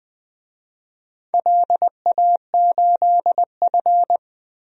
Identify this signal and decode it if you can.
Morse code, heard as a keyed tone, about 20 wpm, 700 Hz. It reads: LA8F
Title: DXCC-100 Call Signs - 4 Characters - 20wpm